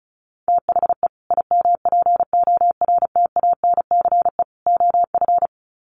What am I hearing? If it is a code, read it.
Morse code, heard as a keyed tone, about 34 wpm, 700 Hz. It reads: THE IMPORTANCE OF
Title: Sets of 3 Words 35wpm